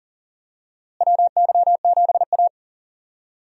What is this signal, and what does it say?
Morse code, heard as a keyed tone, about 40 wpm, 700 Hz. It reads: WY7A
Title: USA Call Signs - 4 Characters - Repeated 3x - 40wpm